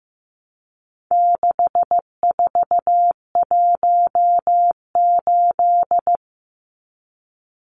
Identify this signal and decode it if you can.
Morse code, heard as a keyed tone, about 15 wpm, 700 Hz. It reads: 6418
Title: Numbers - 4-Digit 15wpm